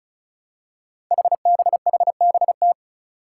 Morse code, heard as a keyed tone, about 36 wpm, 700 Hz. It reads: HBHBT
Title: Five Challenging Letter-Number Combinations 35wpm